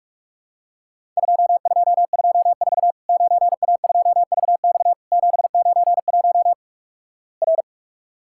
Morse code, heard as a keyed tone, about 46 wpm, 700 Hz. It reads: 2224 9A2VX 791 R
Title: Worked All Europe DX Contest - QTC Exchange - 45wpm